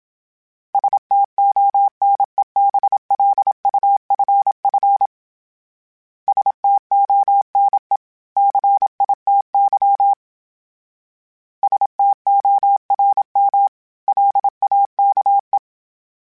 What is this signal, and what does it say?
Morse code, heard as a keyed tone, about 27 wpm, 800 Hz. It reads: STONEBLUFF STONE CITY STORM LAKE
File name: fromSMs / FromSMs mp3s / Cities@25